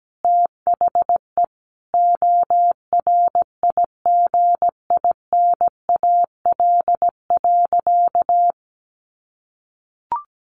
Morse code, heard as a keyed tone, about 17 wpm, 700 Hz. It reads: THE ORIGINAL.
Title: Sentences from Top 900 Words 17wpm